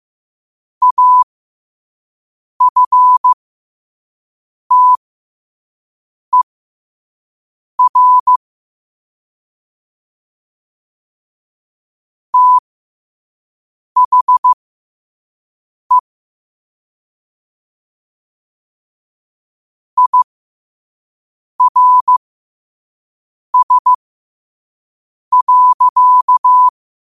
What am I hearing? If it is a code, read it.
Morse code, heard as a keyed tone, about 15 wpm, spaced as for 5 wpm, 1000 Hz. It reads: AFTER THE IRS.
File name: qotdfiles / 5wpm